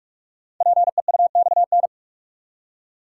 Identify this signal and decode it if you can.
Morse code, heard as a keyed tone, about 44 wpm, 700 Hz. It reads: PEUXN
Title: Code Groups - 5 Letters - 45wpm